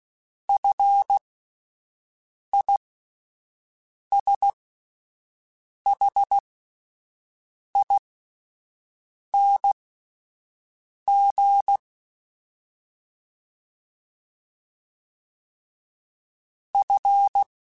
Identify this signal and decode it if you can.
Morse code, heard as a keyed tone, about 16 wpm, spaced as for 5 wpm, 800 Hz. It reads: FISHING F
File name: Fs / Single F mp3s / Single F Prefixes 3x2 Twice @5